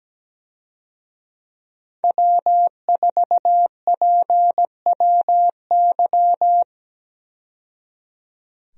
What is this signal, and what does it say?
Morse code, heard as a keyed tone, about 17 wpm, 700 Hz. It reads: W4PWY
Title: USA Call Signs - 5 Characters - 17wpm